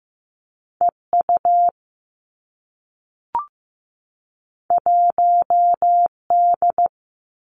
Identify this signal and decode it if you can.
Morse code, heard as a keyed tone, about 15 wpm, 700 Hz. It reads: E U 1 D
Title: Two Challenging Letter-Number Combinations 15wpm